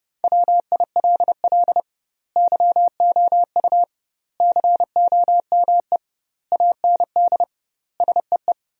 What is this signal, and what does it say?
Morse code, heard as a keyed tone, about 30 wpm, 700 Hz. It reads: WILL YOU COME AND SEE
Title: 2x Word-Spacing - Sentences from Top 100 Words - 30wpm